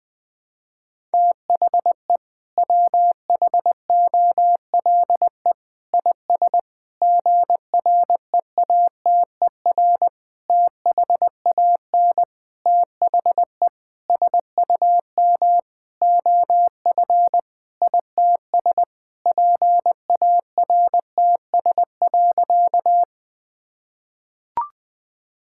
Morse code, heard as a keyed tone, about 20 wpm, 700 Hz. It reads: THE WHOLE IS GREATER THAN THE SUM OF ITS PARTS.
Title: Sentences with 6-Word Phrases - 20wpm